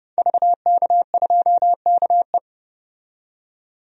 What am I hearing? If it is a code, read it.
Morse code, heard as a keyed tone, about 30 wpm, 700 Hz. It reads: VK2KE
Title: DXCC-100 Call Signs - 5 Characters - 30wpm